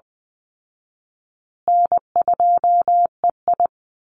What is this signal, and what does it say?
Morse code, heard as a keyed tone, about 19 wpm, 700 Hz. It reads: N2EI
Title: Easy Call Signs Repeated 3x - 20wpm